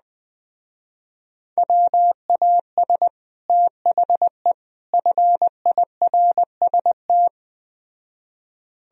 Morse code, heard as a keyed tone, about 20 wpm, 700 Hz. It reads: WAS THE FIRST
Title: Speed-Racing - Sets of 3 Words 17wpm